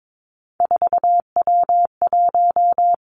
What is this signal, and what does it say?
Morse code, heard as a keyed tone, about 22 wpm, 700 Hz. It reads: 4W1
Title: Three Challenging Letter-Number Combinations 22wpm